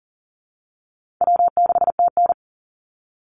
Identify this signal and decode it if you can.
Morse code, heard as a keyed tone, about 40 wpm, 700 Hz. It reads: W6TN
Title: Easy Call Signs Repeated 3x - 40wpm